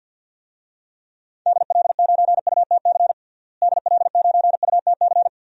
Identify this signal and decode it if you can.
Morse code, heard as a keyed tone, about 53 wpm, 700 Hz. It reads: DL9UTC DL9UTC
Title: DXCC-100 Call Signs - 6 Characters - Repeated 3x - 50wpm